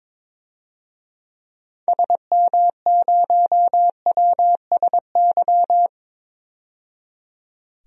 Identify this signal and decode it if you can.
Morse code, heard as a keyed tone, about 22 wpm, 700 Hz. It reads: SM0WSY
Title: DXCC-100 Call Signs - 6 Characters - 22wpm